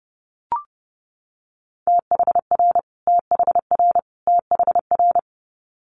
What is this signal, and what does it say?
Morse code, heard as a keyed tone, about 32 wpm, 700 Hz. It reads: THR THR THR
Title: Multi-letter Phonemes Repeated 3x 30wpm